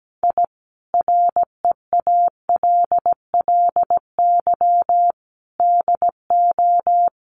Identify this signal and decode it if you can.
Morse code, heard as a keyed tone, about 17 wpm, 700 Hz. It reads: I REALLY DO
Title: Sets of 5 Words Encore 17wpm